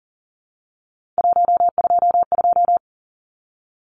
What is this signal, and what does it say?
Morse code, heard as a keyed tone, about 38 wpm, 700 Hz. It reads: 122
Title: Numbers - 3-Digit 40wpm